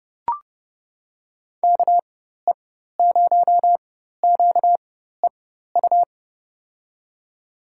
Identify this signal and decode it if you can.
Morse code, heard as a keyed tone, about 32 wpm, spaced as for 14 wpm, 700 Hz. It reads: KE0QEU
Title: USA Call Signs - 6 Characters - 4x Character Spacing - 30wpm